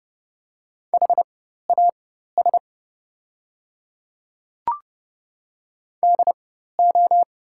Morse code, heard as a keyed tone, about 30 wpm, spaced as for 14 wpm, 700 Hz. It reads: HAS DO
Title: 4x Character Spacing - Top 100 Words - 30wpm